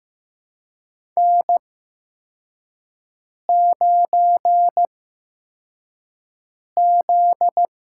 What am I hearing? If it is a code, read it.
Morse code, heard as a keyed tone, about 15 wpm, spaced as for 4 wpm, 700 Hz. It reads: N9Z
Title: USA Call Signs - 4 Characters - 8x Character Spacing - 15wpm